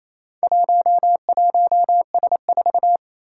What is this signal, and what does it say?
Morse code, heard as a keyed tone, about 27 wpm, 700 Hz. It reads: 11S4
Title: Four Challenging Letter-Number Combinations 28wpm